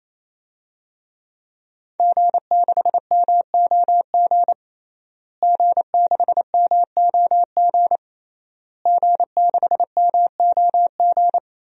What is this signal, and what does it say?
Morse code, heard as a keyed tone, about 28 wpm, 700 Hz. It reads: G6MOG G6MOG G6MOG
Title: DXCC-100 Call Signs - 5 Characters - Repeated 3x - 28wpm